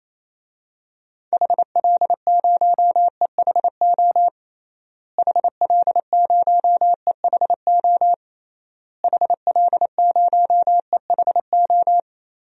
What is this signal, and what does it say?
Morse code, heard as a keyed tone, about 28 wpm, 700 Hz. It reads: HL0EHO HL0EHO HL0EHO
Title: DXCC-100 Call Signs - 6 Characters - Repeated 3x - 28wpm